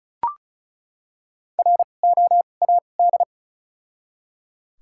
Morse code, heard as a keyed tone, about 36 wpm, 700 Hz. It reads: R O A D
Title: 2x Character Spacing - Top 500 Words - 35wpm